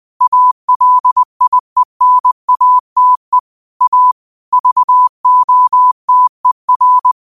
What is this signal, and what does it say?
Morse code, heard as a keyed tone, about 20 wpm, 1000 Hz. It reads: ALIENATE A VOTER